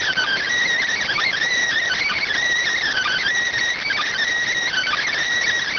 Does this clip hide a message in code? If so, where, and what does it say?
Yes: Emergency Alert System (SAME) tones from 1.1 to 5.1 s, heard as harsh buzzing bursts.